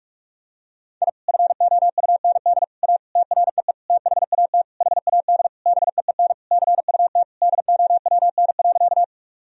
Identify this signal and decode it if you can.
Morse code, heard as a keyed tone, about 41 wpm, 700 Hz. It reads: I FOUND A TREE THAT HAD BEEN CUT DOWN.
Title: Sentences from Top 1000 Words - Review Part 1 - 45wpm